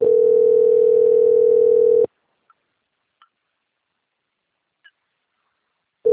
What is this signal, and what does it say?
Ringback tone (North America), 440 + 480 Hz.
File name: out- - - - - .19012